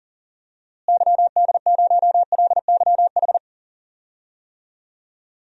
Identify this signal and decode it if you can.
Morse code, heard as a keyed tone, about 40 wpm, 700 Hz. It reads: YD0LYH
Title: DXCC-100 Call Signs - 6 Characters - Repeated 2x - 40wpm